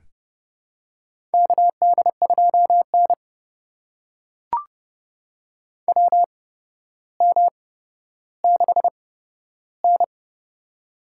Morse code, heard as a keyed tone, about 30 wpm, spaced as for 13 wpm, 700 Hz. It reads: KD2N WM6N
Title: USA Call Signs - 4 Characters - 8x Character Spacing - 30wpm